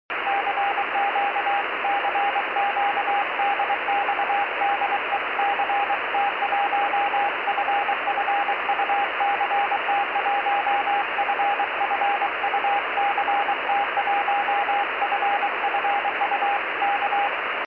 Morse code, heard as a keyed tone, about 23 wpm, 810 Hz, with radio noise. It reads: CQ CQ DX DE CT1FFU CT1FFU CT1FFU K
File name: 70mhz / ct1ffu pe1itr 70195khz